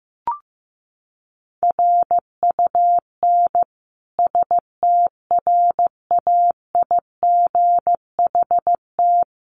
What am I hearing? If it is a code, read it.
Morse code, heard as a keyed tone, about 15 wpm, 700 Hz. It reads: RUN STRAIGHT
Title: Sentences from Top 1000 Words 15wpm